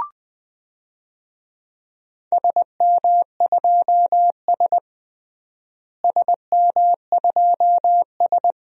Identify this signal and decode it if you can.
Morse code, heard as a keyed tone, about 20 wpm, 700 Hz. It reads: SM2S SM2S
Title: DXCC-100 Call Signs - 4 Characters - Repeated 2x - 20wpm